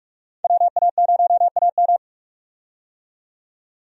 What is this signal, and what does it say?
Morse code, heard as a keyed tone, about 44 wpm, 700 Hz. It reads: WA0AM
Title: USA Call Signs - 5 Characters - 45wpm